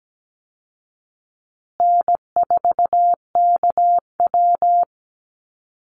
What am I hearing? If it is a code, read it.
Morse code, heard as a keyed tone, about 17 wpm, 700 Hz. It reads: N4KW
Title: Easy Call Signs Repeated 3x - 17wpm